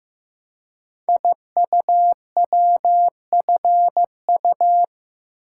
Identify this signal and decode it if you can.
Morse code, heard as a keyed tone, about 15 wpm, 700 Hz. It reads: IUWFU
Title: Code Groups - 5 Letters - 15wpm